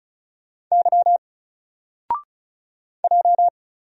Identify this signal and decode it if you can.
Morse code, heard as a keyed tone, about 34 wpm, 700 Hz. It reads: Y J
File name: Single-Letter-Number-35wpm